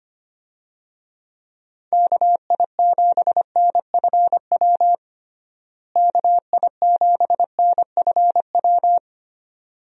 Morse code, heard as a keyed tone, about 25 wpm, 700 Hz. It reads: KI7NFW KI7NFW
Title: USA Call Signs - 6 Characters - Repeated 2x - 25wpm